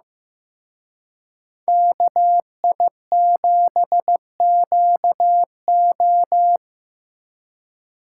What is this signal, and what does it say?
Morse code, heard as a keyed tone, about 18 wpm, 700 Hz. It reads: KI7QO
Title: Speed Racing - USA 2x2 Call Signs 15wpm